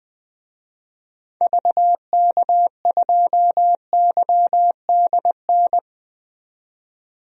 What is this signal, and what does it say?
Morse code, heard as a keyed tone, about 20 wpm, 700 Hz. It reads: VK2YDN
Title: DXCC-100 Call Signs - 6 Characters - 20wpm